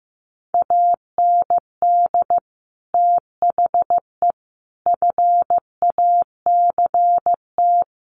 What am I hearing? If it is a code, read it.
Morse code, heard as a keyed tone, about 15 wpm, 700 Hz. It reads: AND THE FACT
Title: Sets of 4 Words 15wpm